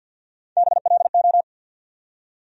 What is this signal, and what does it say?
Morse code, heard as a keyed tone, about 51 wpm, 690 Hz. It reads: BLO
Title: USA Call Sign Postfixes - 3 Letters 50wpm